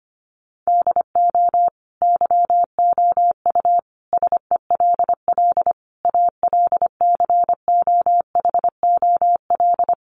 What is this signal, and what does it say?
Morse code, heard as a keyed tone, about 25 wpm, 700 Hz. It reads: DO YOU SELL ALCOHOL
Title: Sentences from Top 1400 Words 25wpm